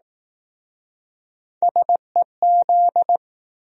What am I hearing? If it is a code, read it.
Morse code, heard as a keyed tone, about 20 wpm, 700 Hz. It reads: SEZ